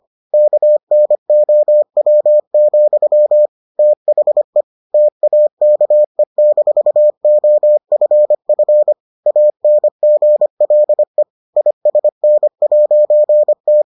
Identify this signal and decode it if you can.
Morse code, heard as a keyed tone, about 25 wpm, 600 Hz. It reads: KNOW, THE TAKE-OFF ANGLE ISN'T